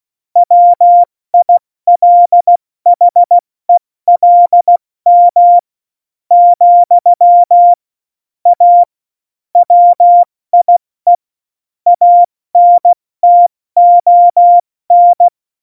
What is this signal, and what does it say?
Morse code, heard as a keyed tone, about 16 wpm, spaced as for 13 wpm, 700 Hz. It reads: WILHELM, A WIE ANTON
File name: Morsekurs / Witze / Witz 05. Juni 2018, 70 F80